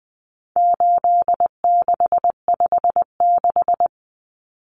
Morse code, heard as a keyed tone, about 20 wpm, 700 Hz. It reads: 8656
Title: Common Cut Numbers - 4-Digit 20wpm